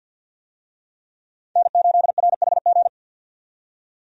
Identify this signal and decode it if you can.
Morse code, heard as a keyed tone, about 52 wpm, 700 Hz. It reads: N8RHG
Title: USA Call Signs - 5 Characters - 50wpm